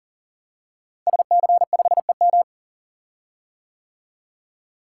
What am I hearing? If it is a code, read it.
Morse code, heard as a keyed tone, about 39 wpm, 700 Hz. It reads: SC5EM
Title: Code Groups - 5 Letter:Numbers - 40wpm